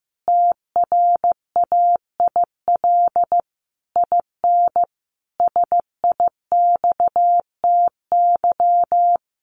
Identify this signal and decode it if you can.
Morse code, heard as a keyed tone, about 15 wpm, 700 Hz. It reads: TRAIL IN SIXTY
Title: A Sound Of Thunder 15wpm